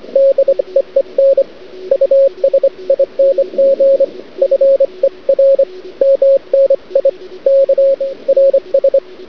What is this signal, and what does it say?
Morse code, heard as a keyed tone, about 25 wpm, 550 Hz, with radio noise. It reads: DEEN USING FER MNI K RS
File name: cd-odj / Suara / cwrag